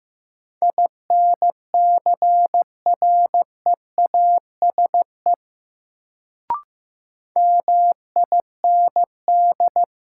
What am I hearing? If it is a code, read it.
Morse code, heard as a keyed tone, about 15 wpm, 700 Hz. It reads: INCREASE MIND
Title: Sets of 5 Words - New Word Drill - 15wpm